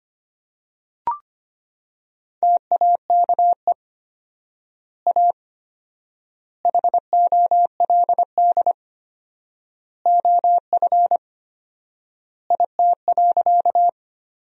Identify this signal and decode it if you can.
Morse code, heard as a keyed tone, about 25 wpm, 700 Hz. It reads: TAKE A HOLD OF IT.
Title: 4x Word-Spacing - Sentences from Top 400 Words - 25wpm